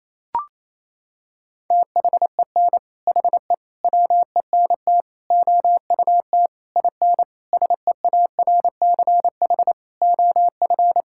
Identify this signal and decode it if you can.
Morse code, heard as a keyed tone, about 28 wpm, 700 Hz. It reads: THEN HE WENT OUT IN SEARCH OF
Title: Sentences from Top 1000 Words - Review Part 5 - 28wpm